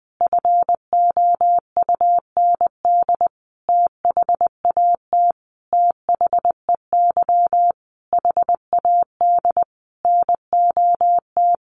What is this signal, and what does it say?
Morse code, heard as a keyed tone, about 20 wpm, 700 Hz. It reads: FOUND THAT THEY HAD NOT
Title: Sentences from Top 1200 Words 20wpm